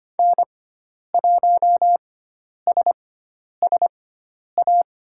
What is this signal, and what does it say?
Morse code, heard as a keyed tone, about 25 wpm, spaced as for 10 wpm, 700 Hz. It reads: N1SSA
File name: Lesson 16g - Letter c - Call Signs 25wpm